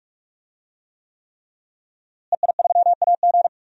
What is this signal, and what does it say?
Morse code, heard as a keyed tone, about 46 wpm, 700 Hz. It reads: EI3AG
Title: DXCC-100 Call Signs - 5 Characters - Repeated 3x - 45wpm